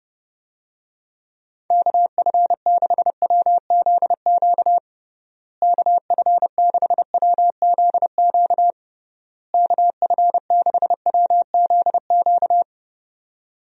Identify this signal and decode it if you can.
Morse code, heard as a keyed tone, about 32 wpm, 700 Hz. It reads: KF6WZQ KF6WZQ KF6WZQ
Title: USA Call Signs - 6 Characters - Repeated 3x - 30wpm